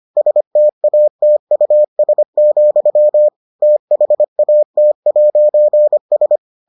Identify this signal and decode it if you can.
Morse code, heard as a keyed tone, about 25 wpm, 600 Hz. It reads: STATUS, THAT'S